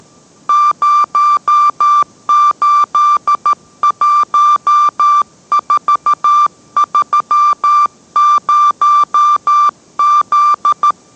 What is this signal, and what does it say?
Morse code, heard as a keyed tone, about 13 wpm, 1180 Hz. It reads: 081430Z